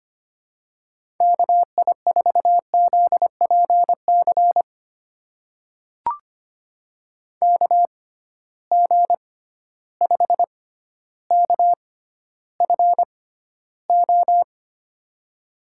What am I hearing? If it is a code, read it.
Morse code, heard as a keyed tone, about 25 wpm, spaced as for 13 wpm, 700 Hz. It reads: KI4ZPC KG5KFO
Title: USA Call Signs - 6 Characters - 6x Character Spacing - 25wpm